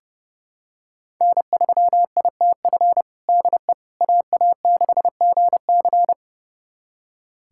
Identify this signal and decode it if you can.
Morse code, heard as a keyed tone, about 30 wpm, 700 Hz. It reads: N3ITF DE AA6GC
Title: USA Callsign DE Callsign - 5 Characters - 30wpm